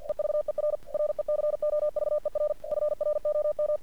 Morse code, heard as a keyed tone, about 47 wpm, 600 Hz. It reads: EVER RECOVER FROM
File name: archived / CRSnet